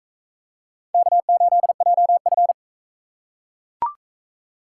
Morse code, heard as a keyed tone, about 42 wpm, 700 Hz. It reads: K8JF